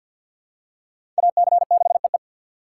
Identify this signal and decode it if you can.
Morse code, heard as a keyed tone, about 48 wpm, 700 Hz. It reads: AC6EE
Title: USA Call Signs - 3x Character Spacing - 50wpm